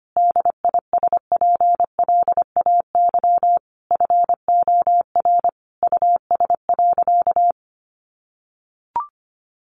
Morse code, heard as a keyed tone, about 25 wpm, 700 Hz. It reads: DISPLAY FOR US.
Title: Sentences from Top 1200 Words 25wpm